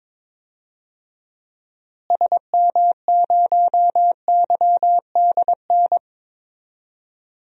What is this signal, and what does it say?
Morse code, heard as a keyed tone, about 22 wpm, 700 Hz. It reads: SM0YDN
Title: DXCC-100 Call Signs - 6 Characters - Repeated 2x - 22wpm